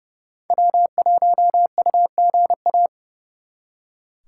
Morse code, heard as a keyed tone, about 38 wpm, 700 Hz. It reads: W1UGA